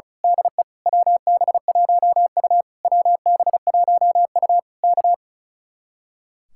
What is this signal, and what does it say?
Morse code, heard as a keyed tone, about 32 wpm, 700 Hz. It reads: DE WB1U WB1U K
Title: Speed-Racing Common QSO Phrases Encore - 35wpm